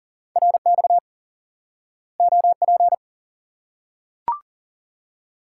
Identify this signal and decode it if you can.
Morse code, heard as a keyed tone, about 40 wpm, 700 Hz. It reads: RX OP
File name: IQE Course 3 - Lesson 89 - YAGI - Two Elements - 40wpm